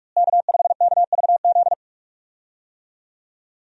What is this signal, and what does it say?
Morse code, heard as a keyed tone, about 46 wpm, 700 Hz. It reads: K5KVZ
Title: USA Call Signs - 5 Characters - Repeated 3x - 45wpm